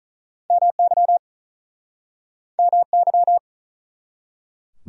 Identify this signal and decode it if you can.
Morse code, heard as a keyed tone, about 45 wpm, 700 Hz. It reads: MY MY